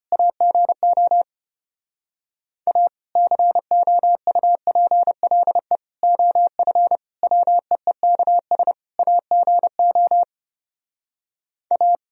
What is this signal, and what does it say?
Morse code, heard as a keyed tone, about 33 wpm, 700 Hz. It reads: AGO A COUPLE OF WEEKS AGO A